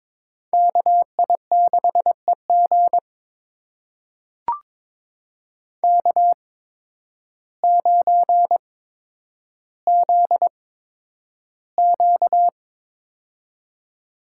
Morse code, heard as a keyed tone, about 23 wpm, spaced as for 10 wpm, 700 Hz. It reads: KI6EG K9ZQ
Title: USA Call Signs - 8x Character Spacing - 22wpm